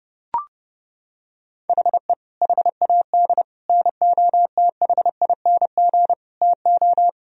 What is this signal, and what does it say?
Morse code, heard as a keyed tone, about 30 wpm, 700 Hz. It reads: HE HAD NOTHING TO
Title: Sentences from Top 1000 Words - Review Part 3 - 30wpm